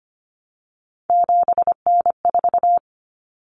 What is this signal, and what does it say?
Morse code, heard as a keyed tone, about 25 wpm, 700 Hz. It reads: 7N4
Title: Common Cut Numbers - 3-Digit 25wpm